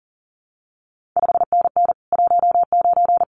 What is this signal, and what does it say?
Morse code, heard as a keyed tone, about 41 wpm, 700 Hz. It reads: 5NN 19
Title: CQ WW DX Contest Exchange Encore - 40wpm